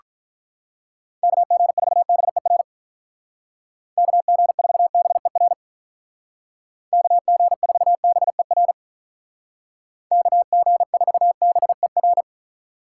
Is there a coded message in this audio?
Yes: Morse code from 1.2 to 12.2 s, heard as a keyed tone, about 47 wpm, 700 Hz. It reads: KG4BER KG4BER KG4BER KG4BER